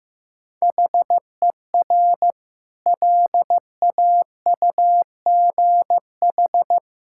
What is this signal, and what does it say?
Morse code, heard as a keyed tone, about 16 wpm, 700 Hz. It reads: HER LAUGH
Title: Sentences from Top 1000 Words - Review Part 5 - 15wpm